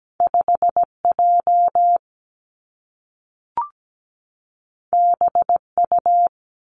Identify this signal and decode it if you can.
Morse code, heard as a keyed tone, about 17 wpm, 700 Hz. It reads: 5J BU
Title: Two Challenging Letter-Number Combinations 17wpm